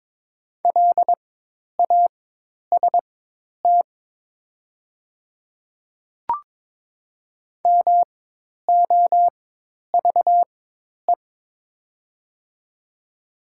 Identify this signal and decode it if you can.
Morse code, heard as a keyed tone, about 22 wpm, spaced as for 10 wpm, 700 Hz. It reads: LAST MOVE
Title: 4x Character Spacing - Top 300 Words - 22wpm